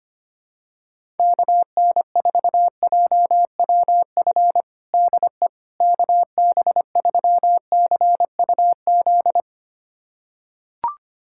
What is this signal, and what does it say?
Morse code, heard as a keyed tone, about 25 wpm, 700 Hz. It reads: KN4JWF DE KB3CUZ
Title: USA Callsign DE Callsign 25wpm